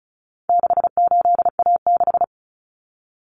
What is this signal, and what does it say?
Morse code, heard as a keyed tone, about 34 wpm, 700 Hz. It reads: 68A6
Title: Common Cut Numbers - 4-Digit 35wpm